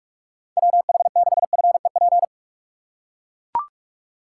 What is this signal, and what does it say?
Morse code, heard as a keyed tone, about 44 wpm, 700 Hz. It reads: WH6FEP
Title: USA Call Signs - 4x Character Spacing - 45wpm